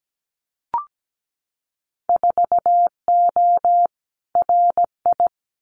Morse code, heard as a keyed tone, about 17 wpm, 700 Hz. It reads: 4O RI
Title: Winter Field Day Exchanges Encore 17wpm